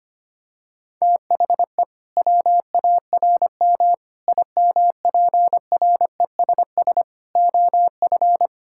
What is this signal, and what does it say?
Morse code, heard as a keyed tone, about 25 wpm, 700 Hz. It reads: THE WARM IMPRESS OF
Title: Sentences from Top 1600 Words 25wpm